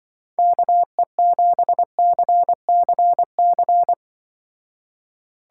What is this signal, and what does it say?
Morse code, heard as a keyed tone, about 23 wpm, 700 Hz. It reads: KE7CCC
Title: Speed Racing - USA 2x3 Call Signs 20wpm